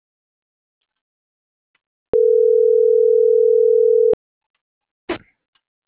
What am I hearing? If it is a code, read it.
Ringback tone (North America), 440 + 480 Hz.